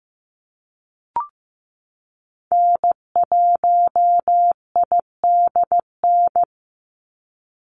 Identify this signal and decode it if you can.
Morse code, heard as a keyed tone, about 15 wpm, 700 Hz. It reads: N1IDN
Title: NA SOTA Activator Callsigns 15wpm